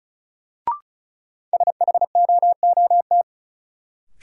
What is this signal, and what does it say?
Morse code, heard as a keyed tone, about 36 wpm, 700 Hz. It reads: SHOOT
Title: Sentences from Top 1200 Words - New Word Drill Alt - 35wpm